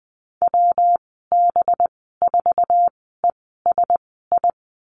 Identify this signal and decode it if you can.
Morse code, heard as a keyed tone, about 20 wpm, spaced as for 14 wpm, 700 Hz. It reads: WB4ESI
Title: 193 Lesson 31g - Letter v - Call Signs 20wpm